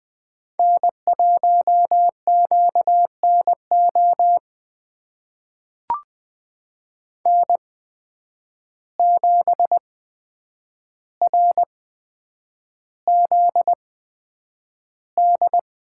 Morse code, heard as a keyed tone, about 20 wpm, spaced as for 9 wpm, 700 Hz. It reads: N1QNO N7RZD
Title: USA Call Signs - 5 Characters - 8x Character Spacing - 20wpm